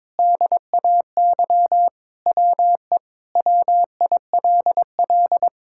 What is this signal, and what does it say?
Morse code, heard as a keyed tone, about 22 wpm, 700 Hz. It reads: DAY WE WILL
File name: practice / 22wpm / Sentences with 3-Word Phrases - 22wpm